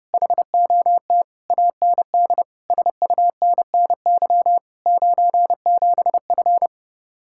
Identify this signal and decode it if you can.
Morse code, heard as a keyed tone, about 30 wpm, 700 Hz. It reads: HOT AND SUNNY 97F